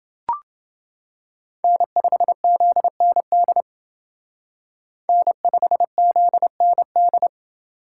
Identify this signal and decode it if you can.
Morse code, heard as a keyed tone, about 29 wpm, 700 Hz. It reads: N5ZND N5ZND